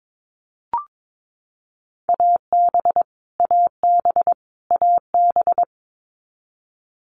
Morse code, heard as a keyed tone, about 22 wpm, 700 Hz. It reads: AB AB AB
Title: Canadian Provinces and Territories - Repeated 3x - 22wpm